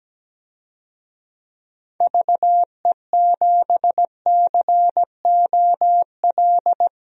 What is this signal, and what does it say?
Morse code, heard as a keyed tone, about 17 wpm, 700 Hz. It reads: VE7COL
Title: DXCC-100 Call Signs - 6 Characters - 17wpm